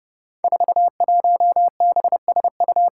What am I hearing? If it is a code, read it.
Morse code, heard as a keyed tone, about 28 wpm, 700 Hz. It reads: 41BSU
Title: Five Challenging Letter-Number Combinations 30wpm